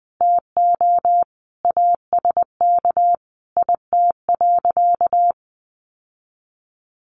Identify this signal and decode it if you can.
Morse code, heard as a keyed tone, about 19 wpm, 700 Hz. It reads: TO ASK IT.
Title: Sentences from Top 1300 Words 20wpm